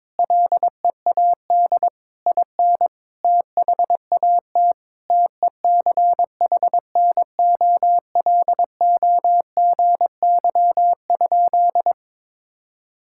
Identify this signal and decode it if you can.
Morse code, heard as a keyed tone, about 21 wpm, 700 Hz. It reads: LEAD IN THAT TECHNOLOGY?
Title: Sentences from Top 1000 Words - Review Part 5 - 22wpm